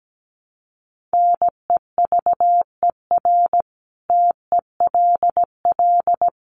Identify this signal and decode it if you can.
Morse code, heard as a keyed tone, about 17 wpm, 700 Hz. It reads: NEVER TELL
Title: Sentences from Top 900 Words 17wpm